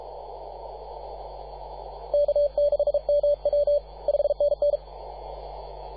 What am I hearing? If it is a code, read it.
Morse code, heard as a keyed tone, about 33 wpm, 580 Hz. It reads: K6MW 5NN
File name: posts / 2015-sept-24-2118z-t2gc-18-071-cw